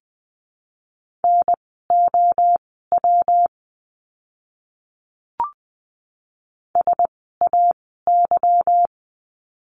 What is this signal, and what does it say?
Morse code, heard as a keyed tone, about 20 wpm, spaced as for 15 wpm, 700 Hz. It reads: NOW SAY
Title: 160 Lesson 26f - Letter g - Words 20wpm